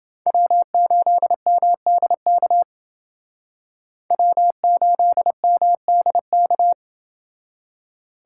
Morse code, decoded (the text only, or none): W8MDK W8MDK